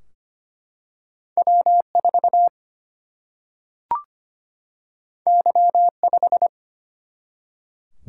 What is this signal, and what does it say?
Morse code, heard as a keyed tone, about 25 wpm, 700 Hz. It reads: W 4 Y 5
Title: Single-Letter Single-Number 25wpm